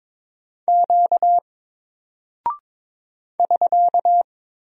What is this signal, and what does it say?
Morse code, heard as a keyed tone, about 22 wpm, 700 Hz. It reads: Q <SK>
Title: Single Character (Letter:Number:Punctuation:Prosign) Alt 22wpm